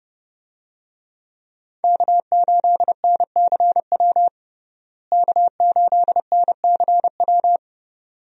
Morse code, heard as a keyed tone, about 30 wpm, 700 Hz. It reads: K8NCW K8NCW
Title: USA Call Signs - 5 Characters - Repeated 2x - 30wpm